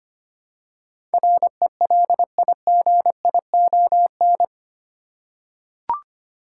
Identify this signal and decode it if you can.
Morse code, heard as a keyed tone, about 25 wpm, 700 Hz. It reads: RELIGION
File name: Eight Letter Words - 25wpm